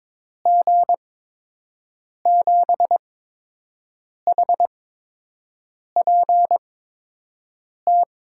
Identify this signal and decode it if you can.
Morse code, heard as a keyed tone, about 22 wpm, spaced as for 6 wpm, 700 Hz. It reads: G7HPT
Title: USA Call Signs - 6 Characters - 8x Character Spacing - 22wpm